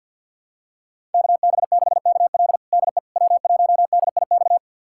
Morse code, heard as a keyed tone, about 50 wpm, 700 Hz. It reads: KB6KL DE W1DIX
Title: USA Callsign DE Callsign 50wpm